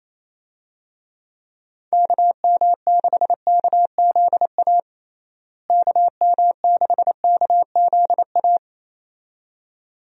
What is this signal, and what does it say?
Morse code, heard as a keyed tone, about 28 wpm, 700 Hz. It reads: KM6KZA KM6KZA
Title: USA Call Signs - 6 Characters - Repeated 2x - 28wpm